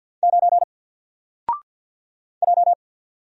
Morse code, heard as a keyed tone, about 50 wpm, 700 Hz. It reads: 9 J
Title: Single Character (Letter:Number:Punctuation:Prosign) 50wpm